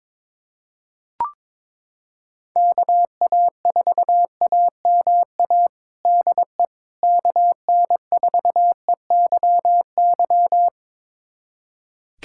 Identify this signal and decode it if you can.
Morse code, heard as a keyed tone, about 21 wpm, 700 Hz. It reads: KA4AMA DE KN4EYY
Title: USA Callsign DE Callsign 22wpm